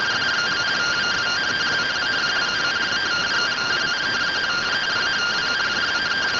Two-tone teleprinter signal, about 170 Hz shift.